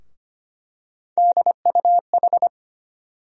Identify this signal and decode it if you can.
Morse code, heard as a keyed tone, about 25 wpm, 700 Hz. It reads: DUH
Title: USA Call Sign Postfixes - 3 Letters 25wpm